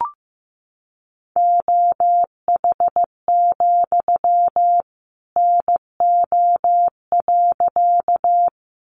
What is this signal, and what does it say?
Morse code, heard as a keyed tone, about 15 wpm, 700 Hz. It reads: OH, NO.
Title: Sentences from Top 500 Words 15wpm